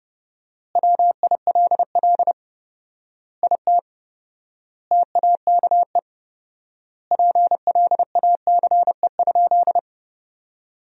Morse code, heard as a keyed tone, about 30 wpm, 700 Hz. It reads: WILL IT TAKE PLACE?
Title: 4x Word-Spacing - Sentences from Top 200 Words - 30wpm